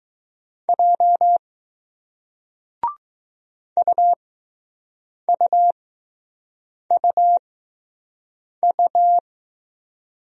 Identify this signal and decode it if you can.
Morse code, heard as a keyed tone, about 21 wpm, spaced as for 6 wpm, 700 Hz. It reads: J UUUU